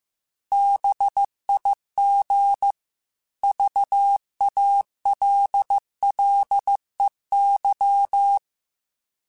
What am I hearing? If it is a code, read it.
Morse code, heard as a keyed tone, about 15 wpm, 800 Hz. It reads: BIG VALLEY
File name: fromSMs / FromSMs mp3s / Cities@13